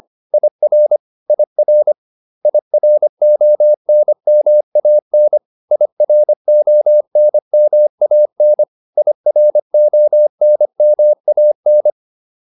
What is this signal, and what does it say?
Morse code, heard as a keyed tone, about 25 wpm, 600 Hz. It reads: IR IR IRONMAN IRONMAN IRONMAN